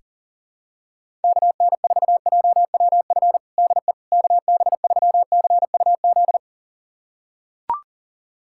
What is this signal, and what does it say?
Morse code, heard as a keyed tone, about 40 wpm, 700 Hz. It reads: KN4JWF DE KB3CUZ
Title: USA Callsign DE Callsign 40wpm